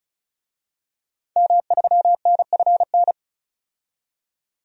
Morse code, heard as a keyed tone, about 34 wpm, 700 Hz. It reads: M3NFN
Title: DXCC-100 Call Signs - 5 Characters - 35wpm